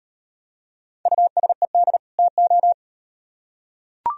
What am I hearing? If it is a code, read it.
Morse code, heard as a keyed tone, about 38 wpm, 700 Hz. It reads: USED TO